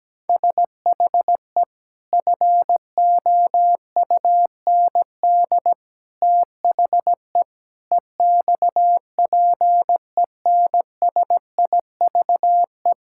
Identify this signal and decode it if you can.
Morse code, heard as a keyed tone, about 17 wpm, 700 Hz. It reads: SHE FOUND THE EXPENSIVE
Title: Sentences from Top 1000 Words - Review Part 6 - 17wpm